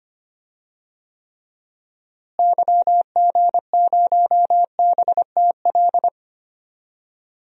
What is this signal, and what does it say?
Morse code, heard as a keyed tone, about 25 wpm, 700 Hz. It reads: YG0BTL
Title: DXCC-100 Call Signs - 6 Characters - Repeated 2x - 25wpm